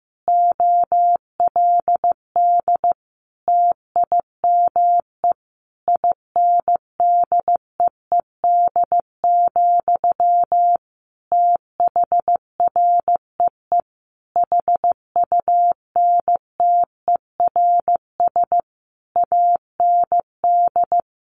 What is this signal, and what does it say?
Morse code, heard as a keyed tone, about 15 wpm, 700 Hz. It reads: OLD TIME INDEED, THREE HUNTERS AND
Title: A Sound Of Thunder 15wpm